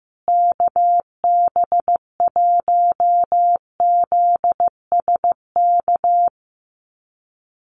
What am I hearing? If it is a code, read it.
Morse code, heard as a keyed tone, about 15 wpm, 700 Hz. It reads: KB1ZSK
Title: USA Call Signs Repeated 2x 15wpm